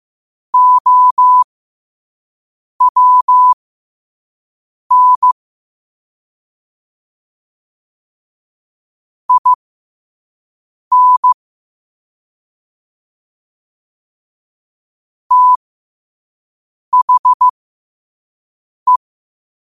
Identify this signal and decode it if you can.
Morse code, heard as a keyed tone, about 15 wpm, spaced as for 5 wpm, 1000 Hz. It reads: OWN IN THE
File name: 5wpm